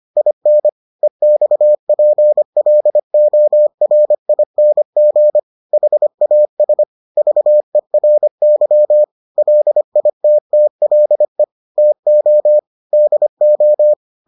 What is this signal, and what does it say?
Morse code, heard as a keyed tone, about 25 wpm, 600 Hz. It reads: IN EXPLORING HAS VERY LITTLE TO DO